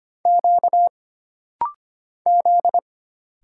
Morse code, heard as a keyed tone, about 25 wpm, 700 Hz. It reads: Q Z
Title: Single Letter-Number - Rapid-Fire 25wpm